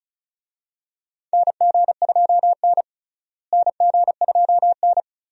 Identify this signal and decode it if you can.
Morse code, heard as a keyed tone, about 36 wpm, 700 Hz. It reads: NG2N NG2N
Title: USA Call Signs - 4 Characters - Repeated 2x - 35wpm